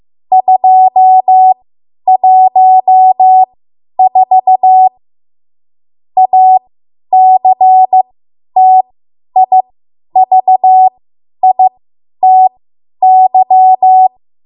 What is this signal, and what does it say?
Morse code, heard as a keyed tone, about 15 wpm, spaced as for 10 wpm, 750 Hz. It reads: 214 ACTIVITY